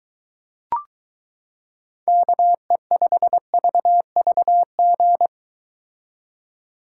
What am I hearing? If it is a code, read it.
Morse code, heard as a keyed tone, about 23 wpm, 700 Hz. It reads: KE5VVG